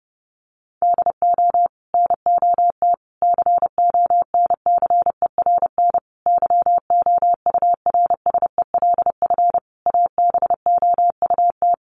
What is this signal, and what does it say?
Morse code, heard as a keyed tone, about 30 wpm, 700 Hz. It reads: DO NOT CONCERN YOURSELF ABOUT
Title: Sentences from Top 1000 Words 30wpm